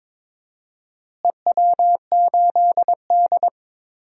Morse code, heard as a keyed tone, about 21 wpm, 700 Hz. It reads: EW8D
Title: DXCC-100 Call Signs - 4 Characters - Repeated 2x - 22wpm